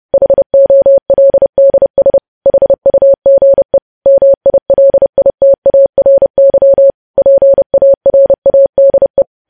Morse code, heard as a keyed tone, about 30 wpm, 550 Hz. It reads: HOLDS HUGE MILITARY PARADE
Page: news headlines in morse code at 30 wpm